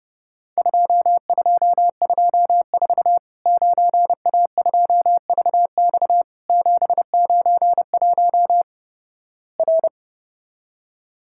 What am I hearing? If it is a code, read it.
Morse code, heard as a keyed tone, about 31 wpm, 700 Hz. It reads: 2224 9A2VX 791 R
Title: Worked All Europe DX Contest - QTC Exchange - 30wpm